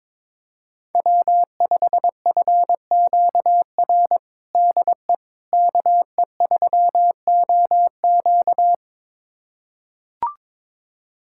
Morse code, heard as a keyed tone, about 22 wpm, 700 Hz. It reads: W5FQR DE KE3OQ
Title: USA Callsign DE Callsign - 5 Characters - 22wpm